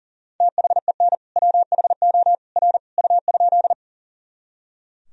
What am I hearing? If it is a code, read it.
Morse code, heard as a keyed tone, about 40 wpm, 700 Hz. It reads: THEN WHO R U?
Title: Sentences from Top 100 Words and Abbreviations 40wpm